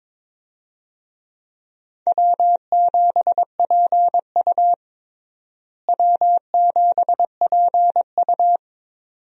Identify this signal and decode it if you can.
Morse code, heard as a keyed tone, about 22 wpm, 700 Hz. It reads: W7PU W7PU
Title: USA Call Signs - 4 Characters - Repeated 2x - 22wpm